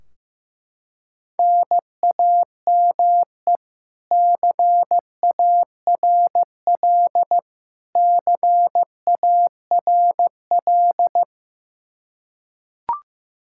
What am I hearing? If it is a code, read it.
Morse code, heard as a keyed tone, about 15 wpm, 700 Hz. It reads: NAME CARL CARL
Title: K1USN SST QSO Practice 15wpm